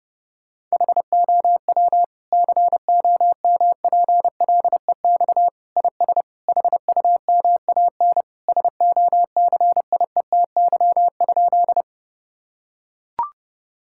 Morse code, heard as a keyed tone, about 32 wpm, 700 Hz. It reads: HOW COMPLEX IS HUMAN SOCIETY?
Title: Sentences from Top 1000 Words - Review Part 6 - 30wpm